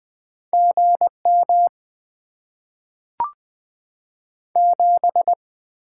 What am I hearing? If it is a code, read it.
Morse code, heard as a keyed tone, about 20 wpm, 700 Hz. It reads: GM 7
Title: Top 20 QSO Elements - 6x Character Spacing - 20wpm